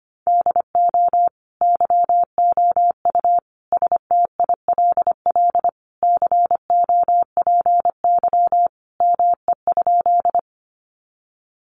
Morse code, heard as a keyed tone, about 25 wpm, 700 Hz. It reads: DO YOU STILL COPY ME?
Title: Sentences from Top 1400 Words 25wpm